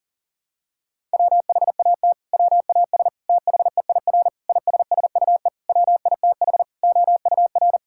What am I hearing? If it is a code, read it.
Morse code, heard as a keyed tone, about 40 wpm, 700 Hz. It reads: WHAT WAS THEIR ISSUE WITH OUR
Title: Sentences from Top 1000 Words - Review Part 6 - 40wpm